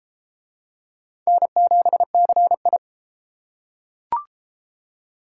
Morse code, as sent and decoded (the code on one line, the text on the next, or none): -. --... -.-. ..
N7CI